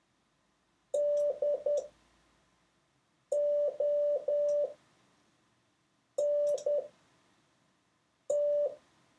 Morse code, heard as a keyed tone, about 10 wpm, spaced as for 4 wpm, 580 Hz. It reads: DONT
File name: 6of6-test